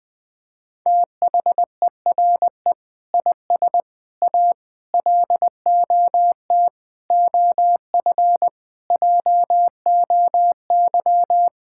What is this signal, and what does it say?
Morse code, heard as a keyed tone, about 19 wpm, 700 Hz. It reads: THERE IS A LOT OF JOY
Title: Sentences with 5-Word Phrases - 20wpm